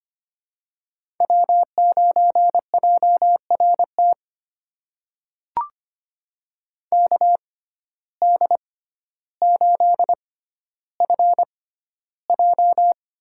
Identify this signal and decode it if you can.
Morse code, heard as a keyed tone, about 25 wpm, spaced as for 13 wpm, 700 Hz. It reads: W9JRT KD8FJ
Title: USA Call Signs - 5 Characters - 6x Character Spacing - 25wpm